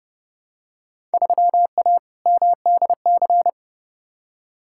Morse code, heard as a keyed tone, about 31 wpm, 700 Hz. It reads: 3A MDC
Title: Field Day Exchanges - 30wpm